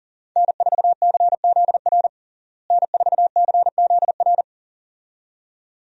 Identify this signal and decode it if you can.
Morse code, heard as a keyed tone, about 40 wpm, 700 Hz. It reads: N4CZR N4CZR
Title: USA Call Signs - 5 Characters - Repeated 2x - 40wpm